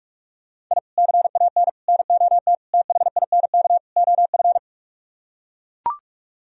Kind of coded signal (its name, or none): Morse code